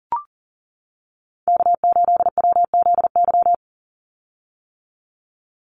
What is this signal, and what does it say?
Morse code, heard as a keyed tone, about 40 wpm, 700 Hz. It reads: K8WZY
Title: USA Call Signs 40wpm